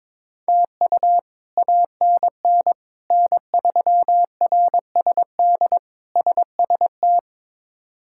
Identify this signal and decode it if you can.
Morse code, heard as a keyed tone, about 22 wpm, 700 Hz. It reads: TU ANN N3RSD SST
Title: K1USN SST QSO Practice - 22wpm